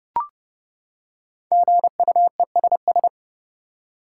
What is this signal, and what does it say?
Morse code, heard as a keyed tone, about 30 wpm, 700 Hz. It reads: GUESS
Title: Five Letter Words - 30wpm